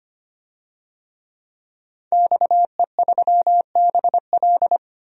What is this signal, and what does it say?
Morse code, heard as a keyed tone, about 25 wpm, 700 Hz. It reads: XE3BL
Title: DXCC-100 Call Signs - 5 Characters - 25wpm